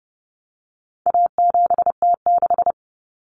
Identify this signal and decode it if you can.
Morse code, heard as a keyed tone, about 30 wpm, 700 Hz. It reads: A7T6
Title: Common Cut Numbers - 4-Digit 30wpm